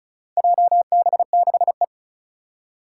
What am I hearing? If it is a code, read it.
Morse code, heard as a keyed tone, about 36 wpm, 690 Hz. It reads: JB6E
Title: Four Challenging Letter-Number Combinations 35wpm